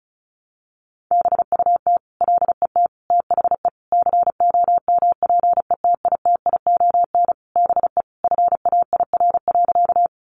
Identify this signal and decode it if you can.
Morse code, heard as a keyed tone, about 34 wpm, 700 Hz. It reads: BUT LET THE COMPETITION BE FAIR.
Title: Sentences from Top 1300 Words 35wpm